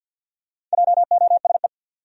Morse code, heard as a keyed tone, about 47 wpm, 700 Hz. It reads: JOSE
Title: USA Common Names - Max 4 Letters - 50wpm